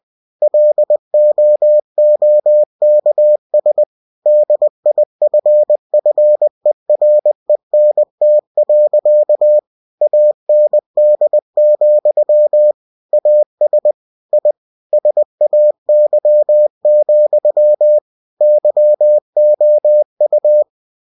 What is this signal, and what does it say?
Morse code, heard as a keyed tone, about 20 wpm, 600 Hz. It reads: LOOKS DIFFERENT. AND, AS I SAY, YOU
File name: cw-books / winnie-the-pooh / winnie-20wpm-page-0006